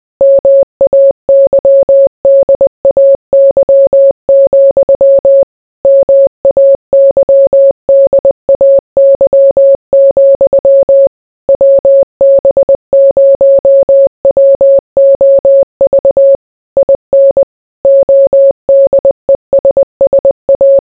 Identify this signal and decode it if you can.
Morse code, heard as a keyed tone, about 20 wpm, 550 Hz. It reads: MAYDAY, MAYDAY, WB0WOV IN ODESSA